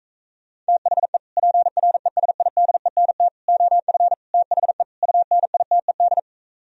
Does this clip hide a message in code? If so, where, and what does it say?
Yes: Morse code from 0.7 to 6.2 s, heard as a keyed tone, about 37 wpm, 700 Hz. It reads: THE PRESIDENT OF THE UNITED